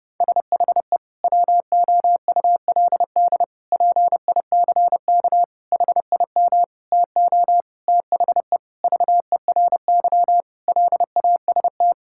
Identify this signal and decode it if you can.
Morse code, heard as a keyed tone, about 32 wpm, 700 Hz. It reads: SHE WOULD PICK HIM TO THE VERY LAST
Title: Sentences from Top 1600 Words 30wpm